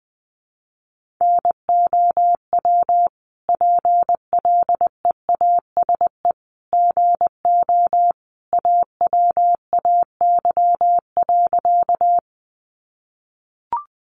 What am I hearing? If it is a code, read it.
Morse code, heard as a keyed tone, about 20 wpm, 700 Hz. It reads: NOW PLEASE GO AWAY.
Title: Sentences from Top 1500 Words 20wpm